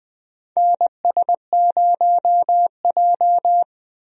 Morse code, heard as a keyed tone, about 20 wpm, 700 Hz. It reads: NS0J
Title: USA Call Signs - 4 Characters - Repeated 2x - 20wpm